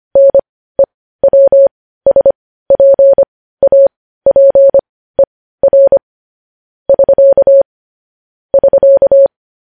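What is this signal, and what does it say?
Morse code, heard as a keyed tone, about 25 wpm, spaced as for 15 wpm, 550 Hz. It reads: NEWSPAPER <SK> <SK>
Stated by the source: Morse code news headlines at 15.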